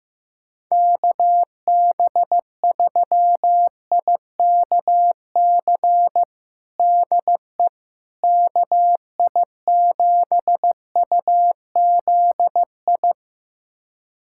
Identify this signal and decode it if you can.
Morse code, heard as a keyed tone, about 15 wpm, 700 Hz. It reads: KB3IKC DE KI7UZI
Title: USA Callsign DE Callsign - 6 Characters - 15wpm